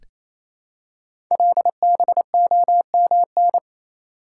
Morse code, heard as a keyed tone, about 27 wpm, 700 Hz. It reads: LBOMN
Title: Code Groups - 5 Letters - 28wpm